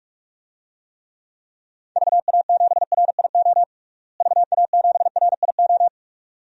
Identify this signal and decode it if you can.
Morse code, heard as a keyed tone, about 46 wpm, 700 Hz. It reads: VA7RIO VA7RIO
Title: DXCC-100 Call Signs - 6 Characters - Repeated 2x - 45wpm